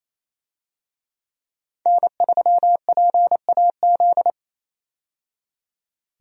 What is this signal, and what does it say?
Morse code, heard as a keyed tone, about 28 wpm, 700 Hz. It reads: N3PAZ
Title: USA Call Signs - 5 Characters - 28wpm